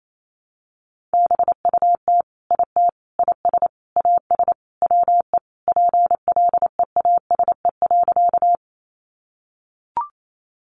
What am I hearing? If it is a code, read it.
Morse code, heard as a keyed tone, about 28 wpm, 700 Hz. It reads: BUT IT IS AS WE PLEASE.
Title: Sentences from Top 1500 Words 28wpm